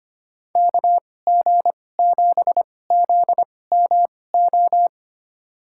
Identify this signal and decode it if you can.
Morse code, heard as a keyed tone, about 25 wpm, spaced as for 18 wpm, 700 Hz. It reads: KG7ZMO
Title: USA Call Signs - 6 Characters - 2x Character Spacing - 25wpm